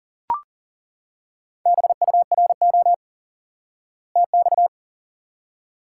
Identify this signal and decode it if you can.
Morse code, heard as a keyed tone, about 40 wpm, 700 Hz. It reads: BURO TX
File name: IQE Course 3 - Lesson 89 - YAGI - Two Elements - 40wpm